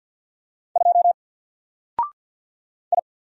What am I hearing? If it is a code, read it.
Morse code, heard as a keyed tone, about 48 wpm, 700 Hz. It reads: 2 I
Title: Single Character (Letter:Number:Punctuation:Prosign) Alt 50wpm